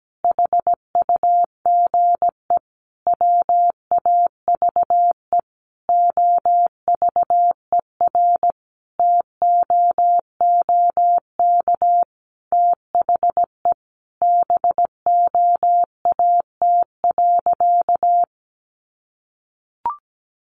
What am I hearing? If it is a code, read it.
Morse code, heard as a keyed tone, about 17 wpm, 700 Hz. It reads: HUGE WAVE OVERTOOK THE BOAT.
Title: Sentences from Top 1400 Words 17wpm